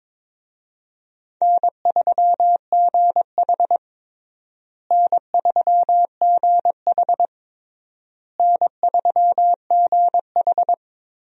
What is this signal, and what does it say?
Morse code, heard as a keyed tone, about 22 wpm, 700 Hz. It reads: N3GH N3GH N3GH
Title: USA Call Signs - 4 Characters - Repeated 3x - 22wpm